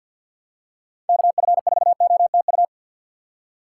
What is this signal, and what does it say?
Morse code, heard as a keyed tone, about 52 wpm, 690 Hz. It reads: KF4OTU
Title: USA Call Signs - 6 Characters - 3x Character Spacing - 50wpm